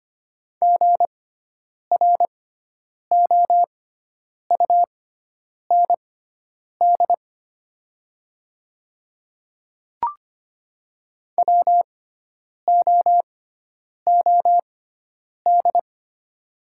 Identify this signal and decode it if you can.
Morse code, heard as a keyed tone, about 25 wpm, spaced as for 9 wpm, 700 Hz. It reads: GROUND WOOD
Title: 6x Character Spacing - Top 500 Words - 25wpm